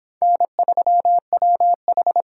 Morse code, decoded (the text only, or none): N3WH